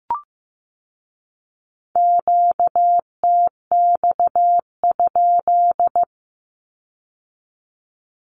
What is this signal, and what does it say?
Morse code, heard as a keyed tone, about 15 wpm, 700 Hz. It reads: QTX?
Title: Q-Codes with Spoken Meaning - 15wpm